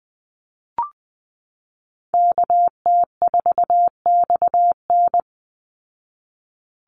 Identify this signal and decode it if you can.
Morse code, heard as a keyed tone, about 19 wpm, 700 Hz. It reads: KT4XN
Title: Call Signs Repeated 3x - 20wpm